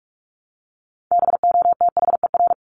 Morse code, heard as a keyed tone, about 44 wpm, 700 Hz. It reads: BOTHER
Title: Top 1501-1750 Words 45wpm